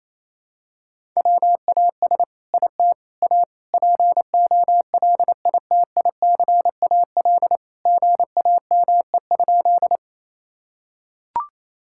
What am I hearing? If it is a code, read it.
Morse code, heard as a keyed tone, about 28 wpm, 700 Hz. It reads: WAS IT A POLITICAL GAME?
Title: Sentences from Top 1000 Words - Review Part 5 - 28wpm